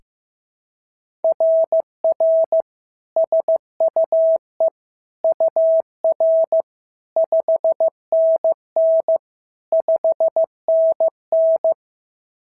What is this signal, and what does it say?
Morse code, heard as a keyed tone, about 15 wpm, 650 Hz. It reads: RR SUE UR 5NN 5NN
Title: SOTA QSO Practice 15wpm